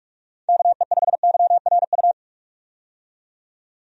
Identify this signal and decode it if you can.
Morse code, heard as a keyed tone, about 46 wpm, 700 Hz. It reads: KE5YRU
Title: USA Call Signs - 6 Characters - 45wpm